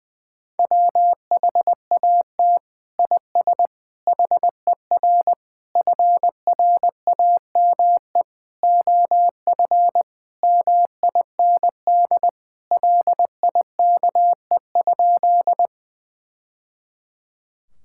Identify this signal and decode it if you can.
Morse code, heard as a keyed tone, about 21 wpm, 700 Hz. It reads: WHAT IS HER FRAME OF MIND LIKE?
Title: Sentences from Top 1000 Words - Review Part 6 - 20wpm